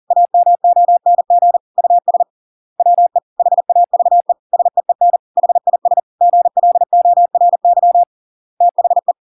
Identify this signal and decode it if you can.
Morse code, heard as a keyed tone, about 40 wpm, 700 Hz. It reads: AMONG US WE HAVE SEEN HIS GLORY THE